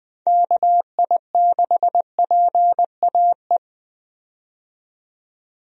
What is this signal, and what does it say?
Morse code, heard as a keyed tone, about 21 wpm, 700 Hz. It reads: KI6PAE
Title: USA Call Signs - 6 Characters - 20wpm